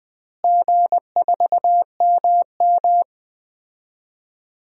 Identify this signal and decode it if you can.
Morse code, heard as a keyed tone, about 20 wpm, 700 Hz. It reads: G4MM
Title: DXCC-100 Call Signs - 4 Characters - Repeated 2x - 20wpm